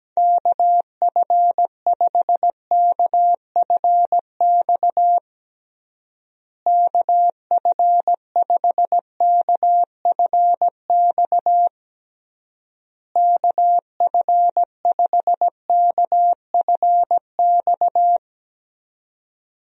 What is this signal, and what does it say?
Morse code, heard as a keyed tone, about 17 wpm, 700 Hz. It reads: KF5KFX KF5KFX KF5KFX
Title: USA Call Signs - 6 Characters - Repeated 3x - 17wpm